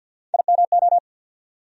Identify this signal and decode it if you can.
Morse code, heard as a keyed tone, about 47 wpm, 700 Hz. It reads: IMO
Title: USA Call Sign Postfixes - 3 Letters 50wpm